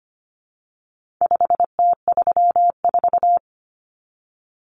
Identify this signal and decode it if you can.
Morse code, heard as a keyed tone, about 25 wpm, 700 Hz. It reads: 5T34
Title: Common Cut Numbers - 4-Digit 25wpm